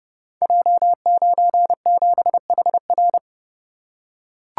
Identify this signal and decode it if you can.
Morse code, heard as a keyed tone, about 30 wpm, 700 Hz. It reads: J97HR
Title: Code Groups - 5 Letter:Numbers - 30wpm